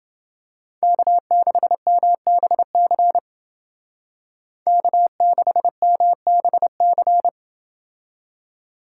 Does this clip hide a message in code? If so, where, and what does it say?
Yes: Morse code from 0.8 to 7.3 s, heard as a keyed tone, about 28 wpm, 700 Hz. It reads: K6MBC K6MBC